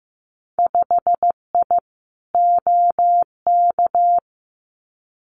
Morse code, heard as a keyed tone, about 15 wpm, 700 Hz. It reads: 5I OK
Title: Winter Field Day Exchanges Encore 15wpm